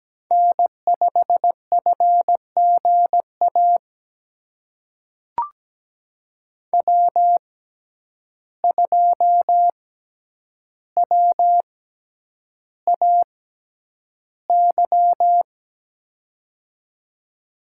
Morse code, heard as a keyed tone, about 17 wpm, spaced as for 9 wpm, 700 Hz. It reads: N5FGA W2WAY
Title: USA Call Signs - 5 Characters - 6x Character Spacing - 17wpm